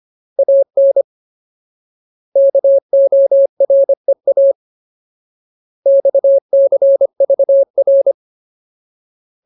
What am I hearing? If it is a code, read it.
Morse code, heard as a keyed tone, about 25 wpm, 550 Hz. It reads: AN KOREA XCVR
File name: Ultimate Word List / 25WPM Ultimate